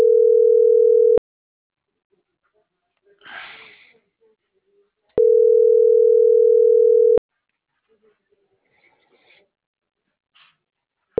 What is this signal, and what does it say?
Ringback tone (North America), 440 + 480 Hz.